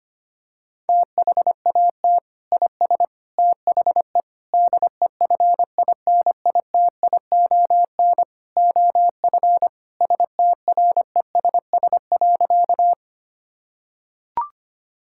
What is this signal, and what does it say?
Morse code, heard as a keyed tone, about 25 wpm, 700 Hz. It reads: THAT IS THE DEFINITION OF STRESS.
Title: Sentences from Top 1000 Words - Review Part 6 - 25wpm